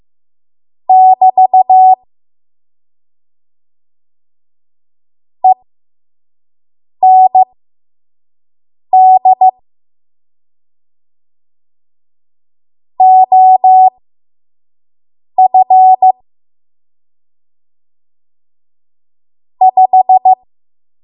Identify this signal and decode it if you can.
Morse code, heard as a keyed tone, about 15 wpm, spaced as for 5 wpm, 750 Hz. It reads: <BT> END OF 5
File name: Morse / 05WPM